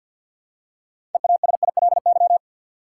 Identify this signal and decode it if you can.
Morse code, heard as a keyed tone, about 53 wpm, 700 Hz. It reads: EASILY
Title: Six Letter Words - 50wpm